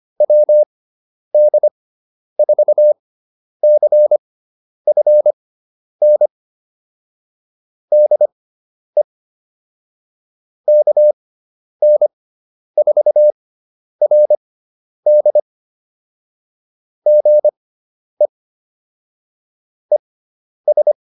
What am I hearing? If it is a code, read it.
Morse code, heard as a keyed tone, about 25 wpm, spaced as for 10 wpm, 600 Hz. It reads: WD4CFN DE KN4RD GE ES